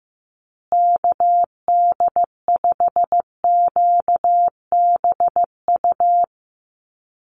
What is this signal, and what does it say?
Morse code, heard as a keyed tone, about 15 wpm, 700 Hz. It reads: KD5QBU
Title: USA Call Signs Repeated 3x 15wpm